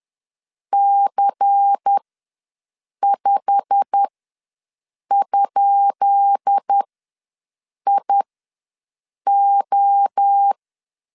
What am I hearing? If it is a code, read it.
Morse code, heard as a keyed tone, about 10 wpm, spaced as for 6 wpm, 800 Hz. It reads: C5?IO
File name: LEZIONI / 10 SIGNS / 10B INT ws 10wpm 4134